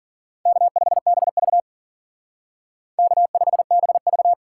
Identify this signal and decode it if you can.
Morse code, heard as a keyed tone, about 47 wpm, 700 Hz. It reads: K5BV K5BV
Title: Speed Racing - USA Call Signs 40wpm